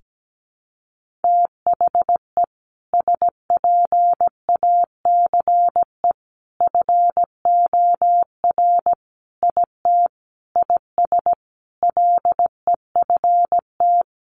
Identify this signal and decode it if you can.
Morse code, heard as a keyed tone, about 17 wpm, 700 Hz. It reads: THE SPACE FOR IT IS LEFT
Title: Sentences from Top 1500 Words 17wpm